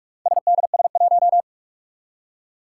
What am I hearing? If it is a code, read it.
Morse code, heard as a keyed tone, about 44 wpm, 700 Hz. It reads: SDS1
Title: Four Challenging Letter-Number Combinations 45wpm